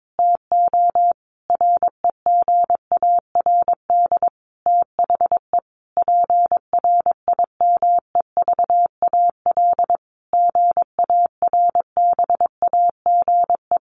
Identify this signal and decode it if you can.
Morse code, heard as a keyed tone, about 22 wpm, 700 Hz. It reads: TO REGARD THE PRIMEVAL GARBAGE
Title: A Sound Of Thunder 22wpm